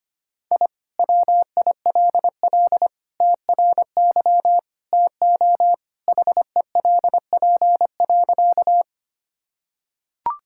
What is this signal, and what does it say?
Morse code, heard as a keyed tone, about 25 wpm, 700 Hz. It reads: I WILL TRY TO HELP.
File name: practice / 25wpm / Sentences with 3-Word Phrases - 25wpm